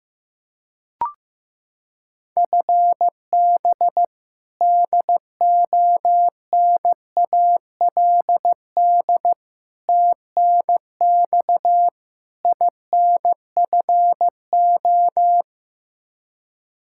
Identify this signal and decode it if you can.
Morse code, heard as a keyed tone, about 15 wpm, 700 Hz. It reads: FB DONALD TNX INFO
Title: Practice QSO - Exchange 3 - More Info - 15wpm